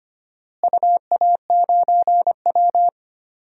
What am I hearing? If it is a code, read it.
Morse code, heard as a keyed tone, about 25 wpm, 700 Hz. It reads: UA9W
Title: DXCC-100 Call Signs - 4 Characters - Repeated 2x - 25wpm